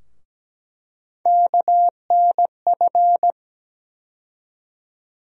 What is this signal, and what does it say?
Morse code, heard as a keyed tone, about 17 wpm, 700 Hz. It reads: KNF
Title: USA Call Sign Postfixes - 3 Letters 17wpm